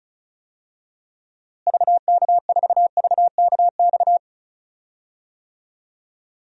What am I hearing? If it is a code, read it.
Morse code, heard as a keyed tone, about 34 wpm, 700 Hz. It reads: VK4VKX
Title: DXCC-100 Call Signs - 6 Characters - Repeated 2x - 35wpm